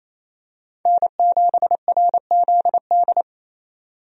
Morse code, heard as a keyed tone, about 28 wpm, 700 Hz. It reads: N7RZD
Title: USA Call Signs - 5 Characters - 28wpm